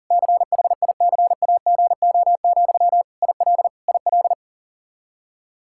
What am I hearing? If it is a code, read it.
Morse code, heard as a keyed tone, about 40 wpm, 700 Hz. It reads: CHICAGO, IL IL
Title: Practice QSO - Exchange 1 - Beginning - 40wpm